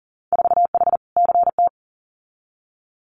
Morse code, heard as a keyed tone, about 42 wpm, 700 Hz. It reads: 4H CT
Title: Winter Field Day Exchanges Encore 40wpm